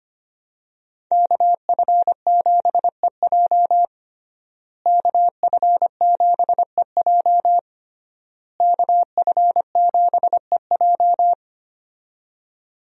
Morse code, heard as a keyed tone, about 25 wpm, 700 Hz. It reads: KF7EJ KF7EJ KF7EJ
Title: USA Call Signs - 5 Characters - Repeated 3x - 25wpm